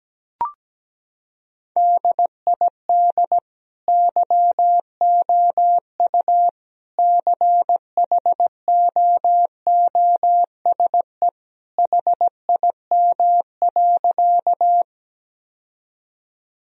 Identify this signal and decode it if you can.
Morse code, heard as a keyed tone, about 17 wpm, 700 Hz. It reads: DID YOU CHOOSE HIM.
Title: Sentences from Top 1000 Words - Review Part 4 - 17wpm